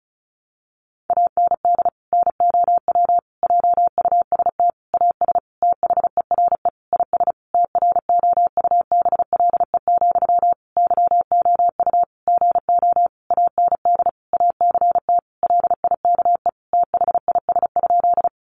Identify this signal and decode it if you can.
Morse code, heard as a keyed tone, about 34 wpm, 700 Hz. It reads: AND NOW JUST AS THERE IS TROUBLE, YOU GO AND ACT LIKE THIS?
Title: Sentences from Top 1200 Words 35wpm